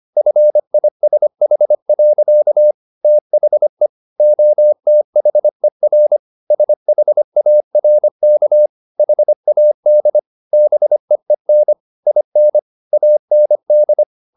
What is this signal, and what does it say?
Morse code, heard as a keyed tone, about 25 wpm, 600 Hz. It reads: FISH. THE OTHER SHARK HAD BEEN IN AND